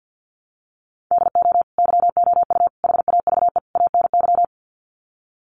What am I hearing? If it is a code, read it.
Morse code, heard as a keyed tone, about 50 wpm, 700 Hz. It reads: DO YOU HAVE ANY
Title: Sets of 4 Words 50wpm